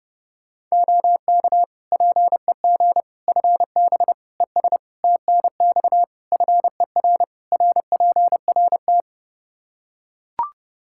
Morse code, heard as a keyed tone, about 30 wpm, 700 Hz. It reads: OK PEG FB ES TNX FER RPRT
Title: Practice QSO - Exchange 2 - Info - 30wpm